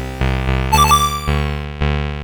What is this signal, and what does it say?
There is no code.